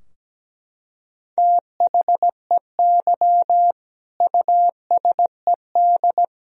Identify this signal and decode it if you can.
Morse code, heard as a keyed tone, about 17 wpm, 700 Hz. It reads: THEY USED
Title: Sentences from Top 1000 Words - Review Part 4 - 17wpm